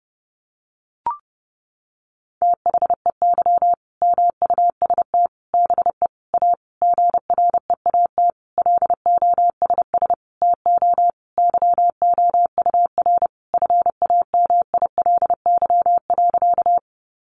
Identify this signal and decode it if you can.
Morse code, heard as a keyed tone, about 30 wpm, 700 Hz. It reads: THEY MUST BE A GREAT LOSS TO YOUR FAMILY.
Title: Sentences from Top 1400 Words 30wpm